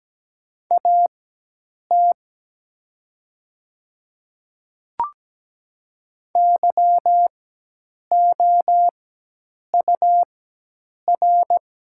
Morse code, heard as a keyed tone, about 17 wpm, spaced as for 8 wpm, 700 Hz. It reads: AT YOUR
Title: 4x Character Spacing - Top 200 Words - 17wpm